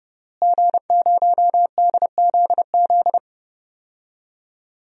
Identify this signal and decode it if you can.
Morse code, heard as a keyed tone, about 30 wpm, 700 Hz. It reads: G0DZZ
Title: DXCC-100 Call Signs - 5 Characters - Repeated 3x - 30wpm